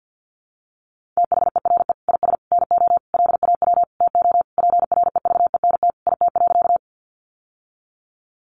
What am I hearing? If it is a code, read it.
Morse code, heard as a keyed tone, about 50 wpm, 700 Hz. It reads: THERE IS NO LAW TO PREVENT IT.
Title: Sentences from Top 1100 Words 50wpm